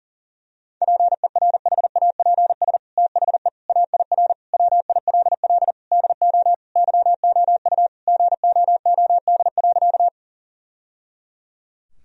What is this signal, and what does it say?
Morse code, heard as a keyed tone, about 37 wpm, 700 Hz. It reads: PERHAPS THE AIR WILL DO YOU GOOD.
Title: Sentences from Top 1000 Words - Review Part 1 - 40wpm